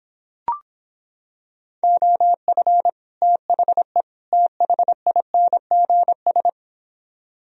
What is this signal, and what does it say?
Morse code, heard as a keyed tone, about 26 wpm, 700 Hz. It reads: OF THE THINGS